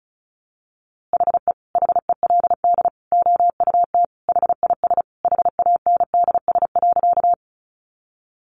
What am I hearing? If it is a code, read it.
Morse code, heard as a keyed tone, about 34 wpm, 700 Hz. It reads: HE HELD OUT HIS HANDS.
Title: A Sound Of Thunder 35wpm